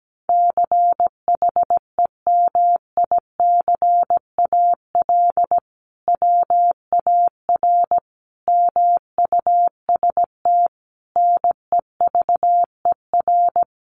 Morse code, heard as a keyed tone, about 17 wpm, 700 Hz. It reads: CHEMICAL WAR MUST NEVER
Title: Sentences from Top 1200 Words 17wpm